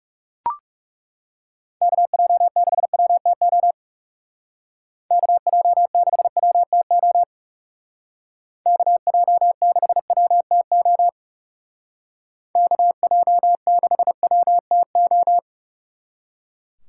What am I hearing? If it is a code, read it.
Morse code, heard as a keyed tone, about 38 wpm, 700 Hz. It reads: KJ6WTO KJ6WTO KJ6WTO KJ6WTO